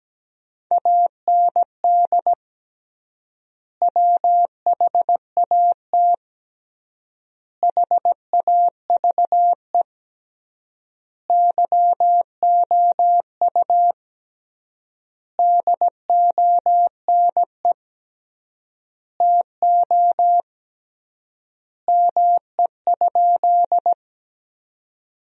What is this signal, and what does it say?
Morse code, heard as a keyed tone, about 17 wpm, 700 Hz. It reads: AND WHAT HAVE YOU DONE TO ME?
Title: 3x Word-Spacing - Sentences from Top 500 Words - 17wpm